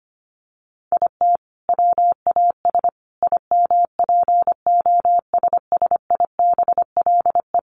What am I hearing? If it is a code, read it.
Morse code, heard as a keyed tone, about 25 wpm, 700 Hz. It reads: IT WAS IMPOSSIBLE
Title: Sentences from Top 1500 Words 25wpm